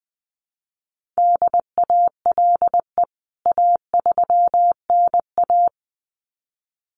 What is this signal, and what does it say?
Morse code, heard as a keyed tone, about 20 wpm, 700 Hz. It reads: DALE A3NA
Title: CWT Exchange 20wpm